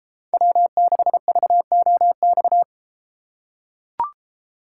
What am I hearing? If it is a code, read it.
Morse code, heard as a keyed tone, about 32 wpm, 700 Hz. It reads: W6VOX